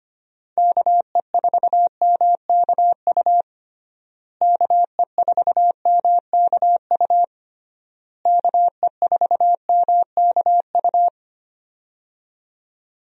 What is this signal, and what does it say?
Morse code, heard as a keyed tone, about 25 wpm, 700 Hz. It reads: KE4MKU KE4MKU KE4MKU
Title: USA Call Signs - 6 Characters - Repeated 3x - 25wpm